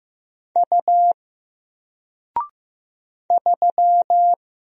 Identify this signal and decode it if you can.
Morse code, heard as a keyed tone, about 15 wpm, 700 Hz. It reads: U 3
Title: Single Letter-Number Alt 15wpm